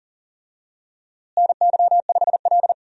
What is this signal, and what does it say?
Morse code, heard as a keyed tone, about 40 wpm, 700 Hz. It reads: NY5L
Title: USA Call Signs - 4 Characters - Repeated 2x - 40wpm